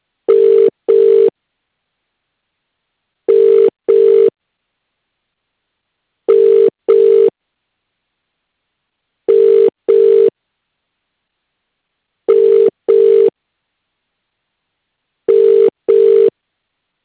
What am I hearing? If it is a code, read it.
Ringback tone (UK), 400 + 450 Hz.